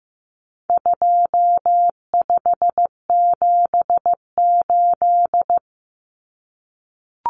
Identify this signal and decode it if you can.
Morse code, heard as a keyed tone, about 15 wpm, 700 Hz. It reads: 2578
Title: Common Cut Numbers - 4-Digit 15wpm